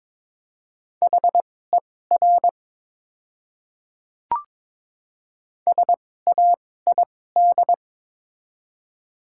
Morse code, heard as a keyed tone, about 22 wpm, spaced as for 16 wpm, 700 Hz. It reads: HER SAID
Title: 2x Character Spacing - Top 100 Words - 22wpm